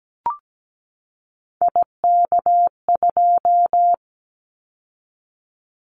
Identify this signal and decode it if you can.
Morse code, heard as a keyed tone, about 17 wpm, 700 Hz. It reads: IK2
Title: Callsign Prefixes for Top 25 DXCC - 17wpm